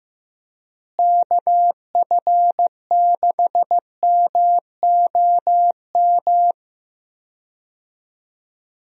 Morse code, heard as a keyed tone, about 15 wpm, 700 Hz. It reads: KF6MOM
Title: USA Call Signs - 6 Characters - Repeated 3x - 15wpm